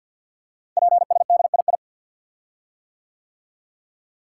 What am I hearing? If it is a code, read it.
Morse code, heard as a keyed tone, about 52 wpm, 700 Hz. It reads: PSDII
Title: Code Groups - 5 Letter:Numbers - 50wpm